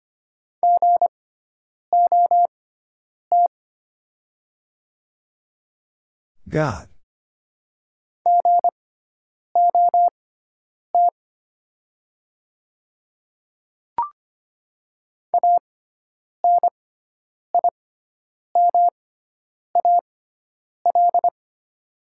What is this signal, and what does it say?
Morse code, heard as a keyed tone, about 25 wpm, spaced as for 9 wpm, 700 Hz. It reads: GOT GOT ANIMAL
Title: 6x Character Spacing - Top 300 Words - 25wpm